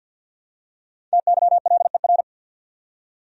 Morse code, heard as a keyed tone, about 52 wpm, 700 Hz. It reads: TYLER
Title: USA Common Names - Max 5 Letters - 50wpm